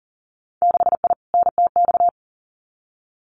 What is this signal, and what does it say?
Morse code, heard as a keyed tone, about 42 wpm, 700 Hz. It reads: 6I NTX
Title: Winter Field Day Exchanges 40wpm